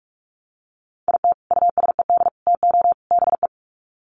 Morse code, heard as a keyed tone, about 44 wpm, 700 Hz. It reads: IT USED TO BE
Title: Sets of 4 Words 45wpm